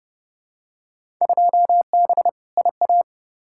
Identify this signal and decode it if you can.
Morse code, heard as a keyed tone, about 30 wpm, 700 Hz. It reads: 2B IA
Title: Field Day Exchanges Encore - 30wpm